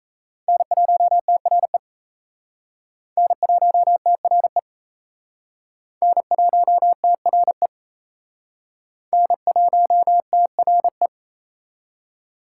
Morse code, heard as a keyed tone, about 34 wpm, 700 Hz. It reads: N1TRE N1TRE N1TRE N1TRE
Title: Speed Racing - USA 1x3 Call Signs 28wpm